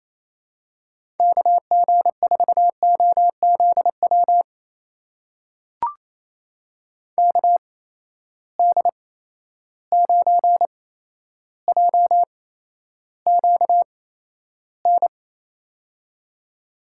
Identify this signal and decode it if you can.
Morse code, heard as a keyed tone, about 28 wpm, spaced as for 12 wpm, 700 Hz. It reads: KG4OZW KD9JQN
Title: USA Call Signs - 8x Character Spacing - 28wpm